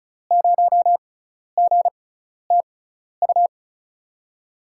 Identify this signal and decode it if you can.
Morse code, heard as a keyed tone, about 36 wpm, spaced as for 12 wpm, 700 Hz. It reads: 0GTU
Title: USA Call Signs - 6x Character Spacing - 35wpm